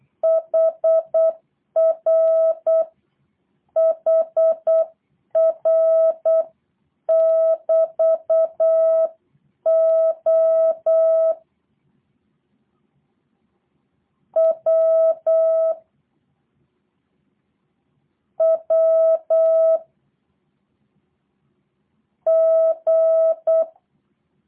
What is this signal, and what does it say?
Morse code, heard as a keyed tone, about 8 wpm, 650 Hz. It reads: HR HR <BT> O W W G
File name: jh8cbh / 2023cw / review1-4